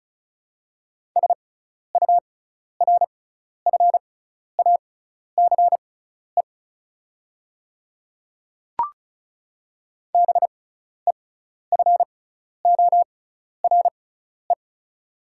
Morse code, heard as a keyed tone, about 36 wpm, spaced as for 12 wpm, 690 Hz. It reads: SURFACE BEFORE
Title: 6x Character Spacing - Top 500 Words - 35wpm